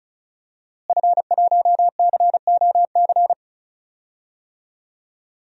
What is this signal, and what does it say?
Morse code, heard as a keyed tone, about 34 wpm, 700 Hz. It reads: F1COC
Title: DXCC-100 Call Signs - 5 Characters - Repeated 2x - 35wpm